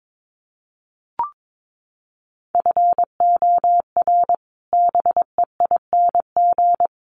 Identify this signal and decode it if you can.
Morse code, heard as a keyed tone, about 22 wpm, 700 Hz. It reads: FOR BEING
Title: Sets of 4 Words 22wpm